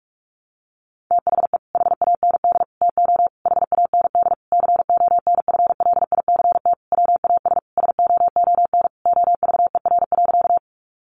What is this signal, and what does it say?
Morse code, heard as a keyed tone, about 44 wpm, 700 Hz. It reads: THE HAND TO HAND CONFLICT WAS SOON OVER.
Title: Sentences from Top 1300 Words 45wpm